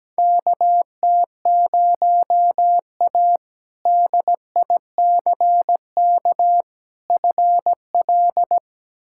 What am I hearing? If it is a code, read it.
Morse code, heard as a keyed tone, about 17 wpm, 700 Hz. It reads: KT0A DICK FL
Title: K1USN SST QSO Practice - 17wpm